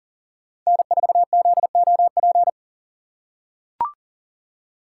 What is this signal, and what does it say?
Morse code, heard as a keyed tone, about 40 wpm, 700 Hz. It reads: N4ZOP
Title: USA Call Signs - 5 Characters - 6x Character Spacing - 40wpm